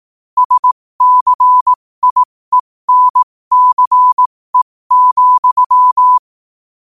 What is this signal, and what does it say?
Morse code, heard as a keyed tone, about 18 wpm, spaced as for 15 wpm, 1000 Hz. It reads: SCIENCE,
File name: qotdfiles / 15wpm